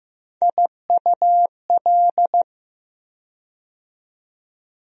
Morse code, heard as a keyed tone, about 15 wpm, 700 Hz. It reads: IUL
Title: USA Call Sign Postfixes - 3 Letters 15wpm